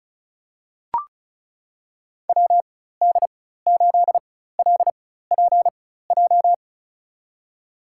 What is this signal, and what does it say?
Morse code, heard as a keyed tone, about 34 wpm, spaced as for 16 wpm, 700 Hz. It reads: WD8LPJ
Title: USA Call Signs - 6 Characters - 4x Character Spacing - 35wpm